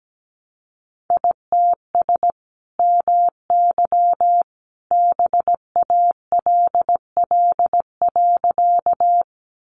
Morse code, heard as a keyed tone, about 17 wpm, 700 Hz. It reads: ITS MY BALL.
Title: Sentences from Top 900 Words 17wpm